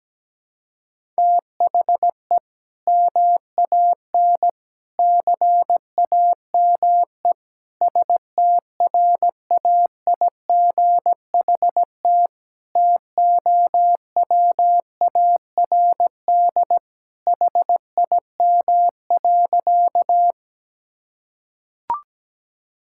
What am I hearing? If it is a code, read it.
Morse code, heard as a keyed tone, about 17 wpm, 700 Hz. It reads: THE MAN CAME STRAIGHT TOWARD HIM.
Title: Sentences from Top 1000 Words - Review Part 6 - 17wpm